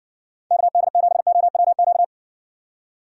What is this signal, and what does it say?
Morse code, heard as a keyed tone, about 57 wpm, 700 Hz. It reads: KD7OWQ